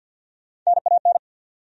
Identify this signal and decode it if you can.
Morse code, heard as a keyed tone, about 53 wpm, 700 Hz. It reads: NAN
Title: USA Common Names - Max 3 Letters - 50wpm